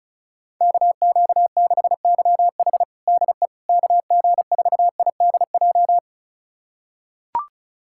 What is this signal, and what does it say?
Morse code, heard as a keyed tone, about 36 wpm, 700 Hz. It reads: KQ6YH DE KG4IDJ
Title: USA Callsign DE Callsign 35wpm